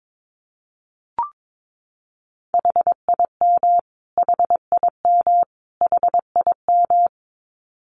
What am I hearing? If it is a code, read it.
Morse code, heard as a keyed tone, about 22 wpm, 700 Hz. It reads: HIM HIM HIM
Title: Top 500 Words Repeated 22wpm